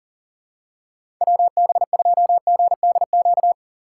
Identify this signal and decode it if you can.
Morse code, heard as a keyed tone, about 40 wpm, 700 Hz. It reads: WB2GDQ
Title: USA Call Signs - 6 Characters - 6x Character Spacing - 40wpm